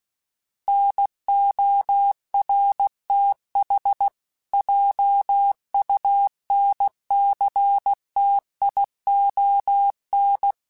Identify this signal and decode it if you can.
Morse code, heard as a keyed tone, about 16 wpm, 800 Hz. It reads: NORTH JUNCTION
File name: fromSMs / FromSMs mp3s / Cities@15